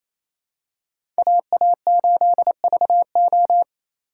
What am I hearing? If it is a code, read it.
Morse code, heard as a keyed tone, about 28 wpm, 700 Hz. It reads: AA8VO
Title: USA Call Signs - 5 Characters - 2x Character Spacing - 28wpm